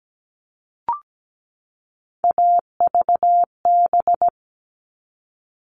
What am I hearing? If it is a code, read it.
Morse code, heard as a keyed tone, about 17 wpm, 700 Hz. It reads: AVB
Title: All Cut Numbers - 3-Digit 17wpm